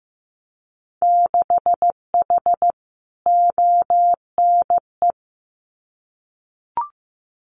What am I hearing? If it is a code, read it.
Morse code, heard as a keyed tone, about 15 wpm, 700 Hz. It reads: 6H ONE
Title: Winter Field Day Exchanges Encore 15wpm